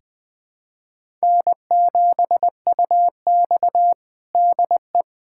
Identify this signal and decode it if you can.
Morse code, heard as a keyed tone, about 21 wpm, 700 Hz. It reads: N7UX DE
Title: USA Callsign DE Callsign - 4 Characters - 20wpm